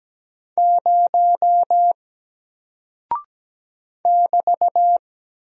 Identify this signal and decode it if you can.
Morse code, heard as a keyed tone, about 17 wpm, 700 Hz. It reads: 0 <BT>
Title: Single Character (Letter:Number:Punctuation:Prosign) Alt 17wpm